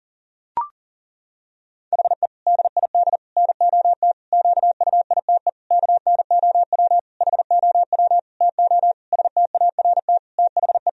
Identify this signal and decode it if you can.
Morse code, heard as a keyed tone, about 37 wpm, 700 Hz. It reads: HE DID NOT QUITE KNOW HOW TO START THE
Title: Sentences from Top 1000 Words - Review Part 1 - 40wpm